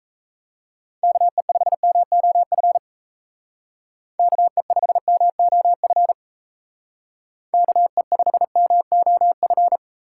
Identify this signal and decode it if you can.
Morse code, heard as a keyed tone, about 39 wpm, 700 Hz. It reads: KE5MOF KE5MOF KE5MOF